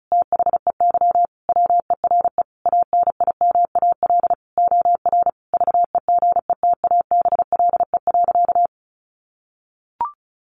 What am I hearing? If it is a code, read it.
Morse code, heard as a keyed tone, about 34 wpm, 700 Hz. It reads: THEY WERE ANIMAL OR VEGETABLE.
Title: Sentences from Top 1300 Words 35wpm